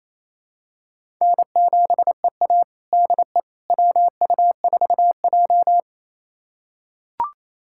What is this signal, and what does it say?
Morse code, heard as a keyed tone, about 27 wpm, 700 Hz. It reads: N7EA DE WU4J
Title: USA Callsign DE Callsign - 4 Characters - 28wpm